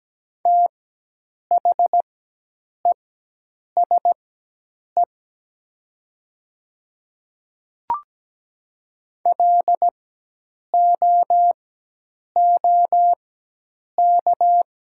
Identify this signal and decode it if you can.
Morse code, heard as a keyed tone, about 17 wpm, spaced as for 8 wpm, 700 Hz. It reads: THESE LOOK
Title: 4x Character Spacing - Top 100 Words - 17wpm